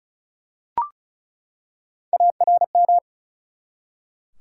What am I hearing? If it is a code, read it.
Morse code, heard as a keyed tone, about 34 wpm, 700 Hz. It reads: ARM
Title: Three Letter Words - 35wpm